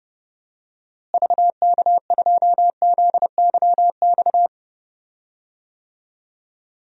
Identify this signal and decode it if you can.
Morse code, heard as a keyed tone, about 30 wpm, 700 Hz. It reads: VK2ZYX
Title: DXCC-100 Call Signs - 6 Characters - Repeated 3x - 30wpm